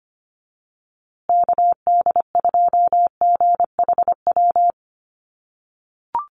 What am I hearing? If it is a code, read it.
Morse code, heard as a keyed tone, about 25 wpm, 700 Hz. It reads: KD2GHW
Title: USA Call Signs Repeated 3x 25wpm